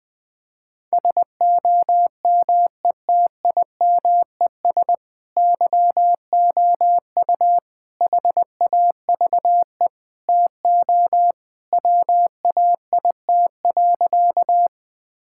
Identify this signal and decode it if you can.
Morse code, heard as a keyed tone, about 21 wpm, 700 Hz. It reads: SOMETIMES YOU HAVE TO WAIT.
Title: Sentences with 3-Word Phrases - 20wpm